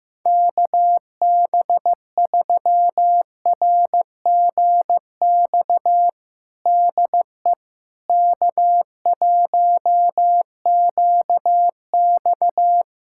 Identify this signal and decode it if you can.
Morse code, heard as a keyed tone, about 16 wpm, 700 Hz. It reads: KB3RGX DE K1QX
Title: Practice QSO - Exchange 4 - Ending - 15wpm